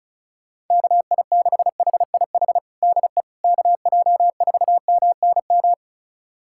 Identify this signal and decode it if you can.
Morse code, heard as a keyed tone, about 36 wpm, 700 Hz. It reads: KI6HIH DE KJ4MNM
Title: USA Callsign DE Callsign 35wpm